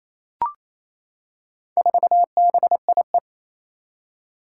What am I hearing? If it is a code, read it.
Morse code, heard as a keyed tone, about 27 wpm, 700 Hz. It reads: 4BIE
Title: Four Challenging Letter-Number Combinations 28wpm